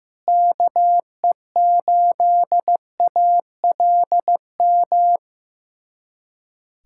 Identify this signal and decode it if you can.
Morse code, heard as a keyed tone, about 15 wpm, 700 Hz. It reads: KE8ALM
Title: USA Callsign DE Callsign 15wpm